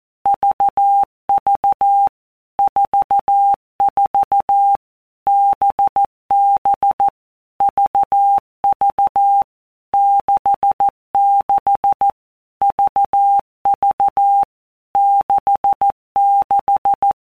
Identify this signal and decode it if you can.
Morse code, heard as a keyed tone, about 14 wpm, 800 Hz. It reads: VV 44 BB VV 66 VV 66
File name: Basic Letter Groups By Blocks / Groups@13 / VB46@13